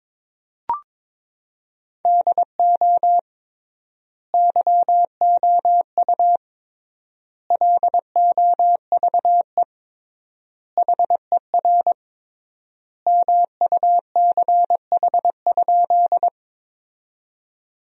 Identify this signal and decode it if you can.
Morse code, heard as a keyed tone, about 22 wpm, 700 Hz. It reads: DO YOU LOVE HER MUCH?
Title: 3x Word-Spacing - Sentences from Top 400 Words - 22wpm